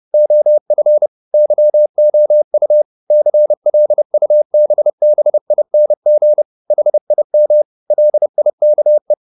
Morse code, heard as a keyed tone, about 30 wpm, 600 Hz. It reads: OF YOU CLUBBING HIM LIKE
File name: the old man and the sea / the old man and the sea 1 30wpm